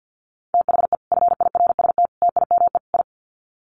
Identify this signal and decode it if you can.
Morse code, heard as a keyed tone, about 50 wpm, 700 Hz. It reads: THE FIRST TIME I
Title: Sets of 4 Words 50wpm